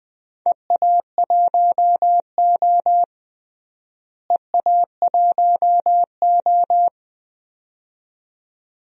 Morse code, heard as a keyed tone, about 20 wpm, 700 Hz. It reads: EA1O EA1O
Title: DXCC-100 Call Signs - 4 Characters - Repeated 2x - 20wpm